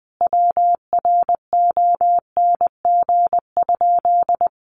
Morse code, heard as a keyed tone, about 20 wpm, 700 Hz. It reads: WRONG?
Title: Sentences from Top 900 Words 20wpm